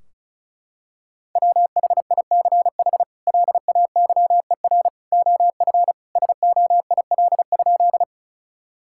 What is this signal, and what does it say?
Morse code, heard as a keyed tone, about 36 wpm, 700 Hz. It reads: WHICH LAYER OF SOIL?
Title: Sentences from Top 1600 Words 35wpm